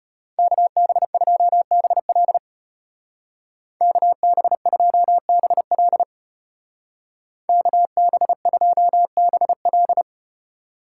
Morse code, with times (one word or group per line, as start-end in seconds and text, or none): 0.4-2.4 KB2BL
3.8-6.0 KB2BL
7.5-10.0 KB2BL